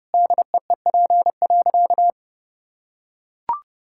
Morse code, heard as a keyed tone, about 30 wpm, 700 Hz. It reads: DEEP.
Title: Sentences from Top 1600 Words 30wpm